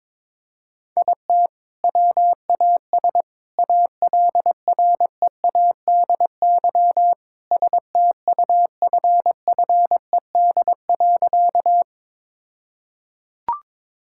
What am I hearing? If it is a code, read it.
Morse code, heard as a keyed tone, about 22 wpm, 700 Hz. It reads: IT WAS ALREADY STUFFED.
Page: morse-code-ninja-practice-22wpm